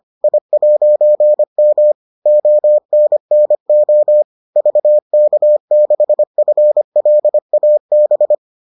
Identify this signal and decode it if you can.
Morse code, heard as a keyed tone, about 25 wpm, 600 Hz. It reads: I'M ONNO VK6FLAB